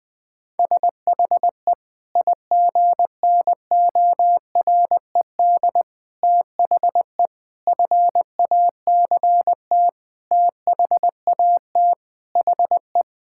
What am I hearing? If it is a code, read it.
Morse code, heard as a keyed tone, about 20 wpm, 700 Hz. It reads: SHE IGNORED THE FACT THAT HE
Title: Sentences with 5-Word Phrases - 20wpm